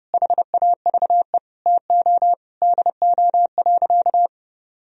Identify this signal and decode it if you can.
Morse code, heard as a keyed tone, about 30 wpm, 700 Hz. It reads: HAVE TO DO.
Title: Sentences from Top 1000 Words - Review Part 2 - 30wpm